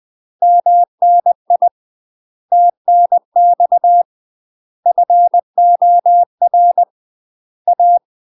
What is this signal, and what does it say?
Morse code, heard as a keyed tone, about 20 wpm, 700 Hz. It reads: MNI TNX FOR A